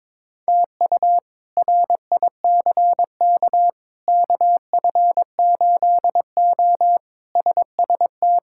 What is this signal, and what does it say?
Morse code, heard as a keyed tone, about 22 wpm, 700 Hz. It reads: TU RICK KF8O SST
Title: K1USN SST QSO Practice - 22wpm